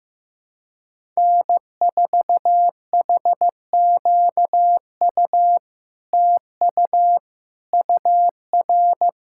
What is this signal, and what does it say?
Morse code, heard as a keyed tone, about 15 wpm, 700 Hz. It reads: N4HQU TU UR
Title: SOTA QSO Practice 15wpm